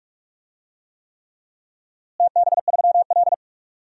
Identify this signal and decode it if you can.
Morse code, heard as a keyed tone, about 46 wpm, 700 Hz. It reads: TB3L
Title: DXCC-100 Call Signs - 4 Characters - Repeated 3x - 45wpm